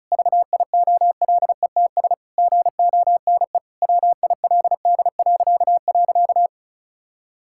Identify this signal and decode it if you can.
Morse code, heard as a keyed tone, about 36 wpm, 700 Hz. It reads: VIOLETS GONE WILD..
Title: Pangram Challenge v2 - 35wpm